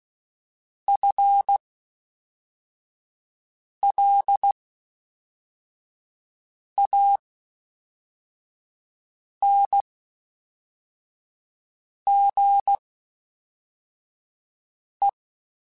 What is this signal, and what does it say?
Morse code, heard as a keyed tone, about 16 wpm, spaced as for 4 wpm, 800 Hz. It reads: FLANGE
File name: Spaced / FL Runs sp mp3s / FL Runs sp @5